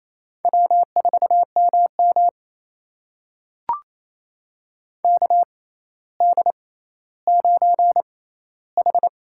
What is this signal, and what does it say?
Morse code, heard as a keyed tone, about 28 wpm, spaced as for 14 wpm, 700 Hz. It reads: W4MM KD9H
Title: USA Call Signs - 4 Characters - 6x Character Spacing - 28wpm